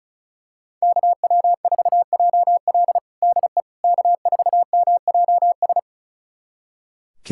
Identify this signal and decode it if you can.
Morse code, heard as a keyed tone, about 36 wpm, 700 Hz. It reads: KW4JL DE K4MJS
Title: USA Callsign DE Callsign - 5 Characters - 35wpm